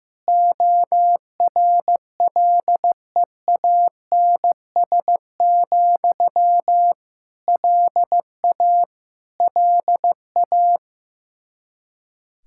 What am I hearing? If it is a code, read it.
Morse code, heard as a keyed tone, about 15 wpm, 700 Hz. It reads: ORLEANS, LA LA
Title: Practice QSO - Exchange 1 - Beginning - 15wpm